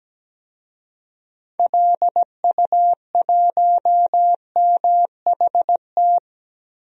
Morse code, heard as a keyed tone, about 17 wpm, 700 Hz. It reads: LU1MHT
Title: DXCC-100 Call Signs - 6 Characters - 17wpm